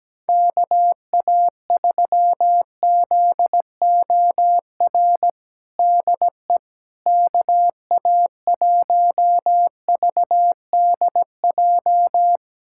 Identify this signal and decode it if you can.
Morse code, heard as a keyed tone, about 17 wpm, 700 Hz. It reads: KA3ZOR DE KA1VDJ
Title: USA Callsign DE Callsign - 6 Characters - 17wpm